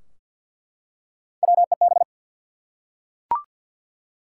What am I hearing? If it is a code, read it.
Morse code, heard as a keyed tone, about 48 wpm, 700 Hz. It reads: WEB
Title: Three Letter Words - 50wpm